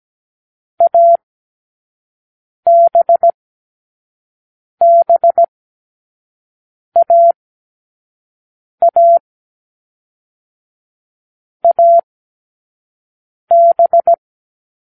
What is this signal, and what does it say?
Morse code, heard as a keyed tone, about 17 wpm, 680 Hz. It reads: A B B A A A B